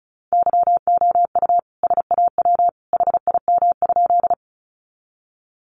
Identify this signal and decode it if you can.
Morse code, heard as a keyed tone, about 34 wpm, 700 Hz. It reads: YOU SAW HIM?
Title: Sentences from Top 300 Words 35wpm